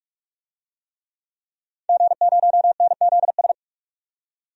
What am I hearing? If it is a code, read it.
Morse code, heard as a keyed tone, about 43 wpm, 700 Hz. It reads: G0NZS
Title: DXCC-100 Call Signs - 5 Characters - 45wpm